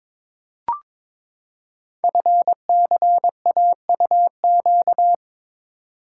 Morse code, heard as a keyed tone, about 22 wpm, 700 Hz. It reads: FCAUQ
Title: Code Groups - 5 Letters - 22wpm